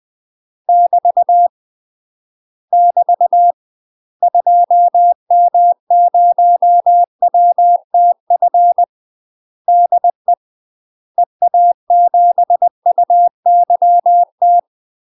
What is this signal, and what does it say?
Morse code, heard as a keyed tone, about 20 wpm, 700 Hz. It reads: <BT> <BT> 2M0WTF DE EA7UYT